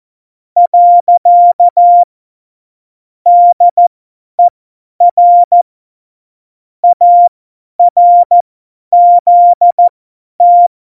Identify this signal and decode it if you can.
Morse code, heard as a keyed tone, about 14 wpm, spaced as for 10 wpm, 700 Hz. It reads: . DER ARZT
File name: Morsekurs / Witze / Witz 09. Januar 2018, 50 F70